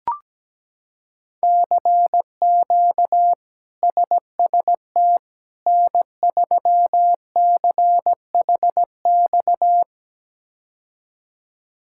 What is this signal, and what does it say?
Morse code, heard as a keyed tone, about 17 wpm, 700 Hz. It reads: CQ SST N3CHX
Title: K1USN SST QSO Practice - 17wpm